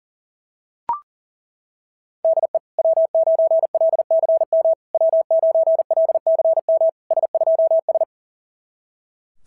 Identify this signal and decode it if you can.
Morse code, heard as a keyed tone, about 37 wpm, 650 Hz. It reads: DE W9LCM W9LCM S2S
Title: SOTA QSO Practice 40wpm